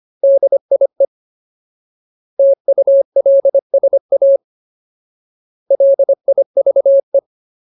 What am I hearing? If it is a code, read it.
Morse code, heard as a keyed tone, about 25 wpm, 550 Hz. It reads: DIE TULSA LIVE
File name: Ultimate Word List / 25WPM Ultimate